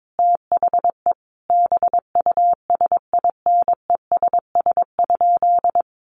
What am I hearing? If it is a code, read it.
Morse code, heard as a keyed tone, about 22 wpm, 700 Hz. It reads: THE BUSINESS?
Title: Sentences from Top 1400 Words 22wpm